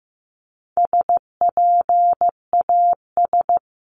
Morse code, heard as a keyed tone, about 15 wpm, 700 Hz. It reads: SPAS
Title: CWT Exchange 15wpm